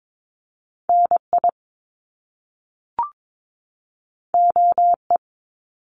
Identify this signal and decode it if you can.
Morse code, heard as a keyed tone, about 22 wpm, 700 Hz. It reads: NI OE
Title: Top Two Letter Combinations in Words 22wpm